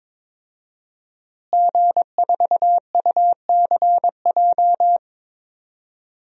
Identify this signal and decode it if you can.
Morse code, heard as a keyed tone, about 22 wpm, 700 Hz. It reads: G4UCJ
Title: DXCC-100 Call Signs - 5 Characters - Repeated 2x - 22wpm